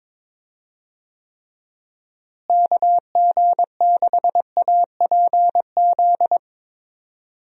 Morse code, heard as a keyed tone, about 21 wpm, 700 Hz. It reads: KG6APZ
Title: USA Call Signs - 6 Characters - Repeated 3x - 22wpm